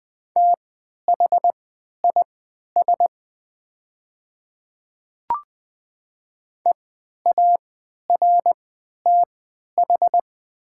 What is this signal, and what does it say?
Morse code, heard as a keyed tone, about 20 wpm, spaced as for 11 wpm, 700 Hz. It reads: THIS EARTH
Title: 3x Character Spacing - Top 200 Words - 20wpm